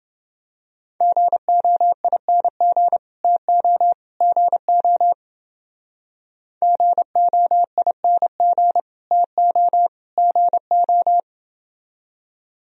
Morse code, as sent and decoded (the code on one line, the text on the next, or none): --. --- .. -. --. / - --- / --. --- / --. --- .. -. --. / - --- / --. ---
GOING TO GO GOING TO GO